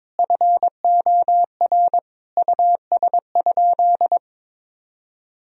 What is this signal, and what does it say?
Morse code, heard as a keyed tone, about 22 wpm, 700 Hz. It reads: FOR US?
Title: Sentences from Top 1000 Words - Review Part 3 - 22wpm